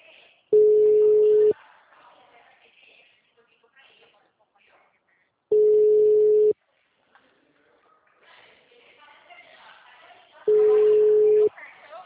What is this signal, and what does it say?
Ringback tone (Europe, 425 Hz), with speech.